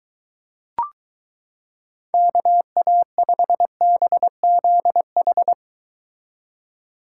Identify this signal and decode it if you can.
Morse code, heard as a keyed tone, about 23 wpm, 700 Hz. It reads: KA5BZH